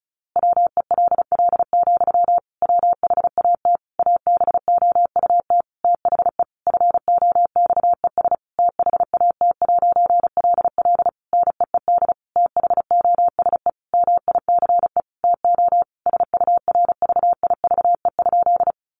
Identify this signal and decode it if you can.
Morse code, heard as a keyed tone, about 34 wpm, 700 Hz. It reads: WELL, WHAT ABOUT THE FOXES THAT'LL NEED THOSE MICE TO SURVIVE?
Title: A Sound Of Thunder 35wpm